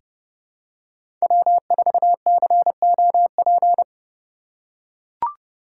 Morse code, heard as a keyed tone, about 29 wpm, 700 Hz. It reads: W4COP
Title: USA Call Signs - 5 Characters - 4x Character Spacing - 30wpm